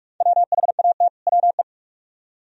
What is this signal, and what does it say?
Morse code, heard as a keyed tone, about 58 wpm, 690 Hz. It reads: WHAT WE